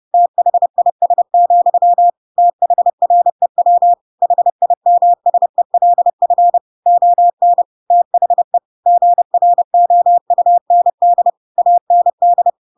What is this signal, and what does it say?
Morse code, heard as a keyed tone, about 30 wpm, 700 Hz. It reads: THIS, THREW HIMSELF ON THE GROUND AND
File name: Pinocchio / Pinocchio ch3 30 30 ik1hge